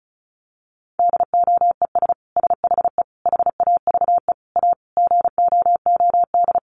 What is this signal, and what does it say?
Morse code, heard as a keyed tone, about 34 wpm, 700 Hz. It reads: DOES SHE HAVE A GOOD
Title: Sentences from Top 1400 Words 35wpm